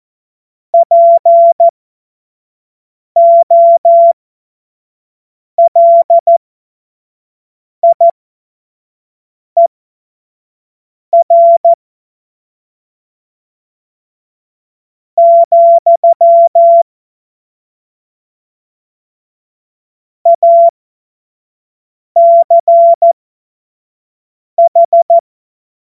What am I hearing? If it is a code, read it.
Morse code, heard as a keyed tone, about 14 wpm, spaced as for 4 wpm, 680 Hz. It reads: POLIER, ACH
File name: Morsekurs / Witze / Witz 30. Dezember 2025 25 F70